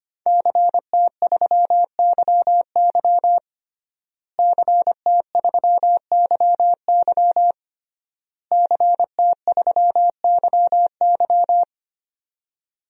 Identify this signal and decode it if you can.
Morse code, heard as a keyed tone, about 25 wpm, 700 Hz. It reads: CT3YY CT3YY CT3YY
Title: DXCC-100 Call Signs - 5 Characters - Repeated 3x - 25wpm